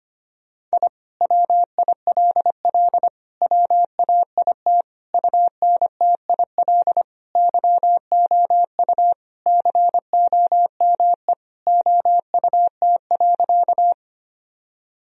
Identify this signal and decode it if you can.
Morse code, heard as a keyed tone, about 25 wpm, 700 Hz. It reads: I WILL WAIT UNTIL YOU COME OUT.
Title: Sentences from Top 1000 Words - Review Part 1 - 25wpm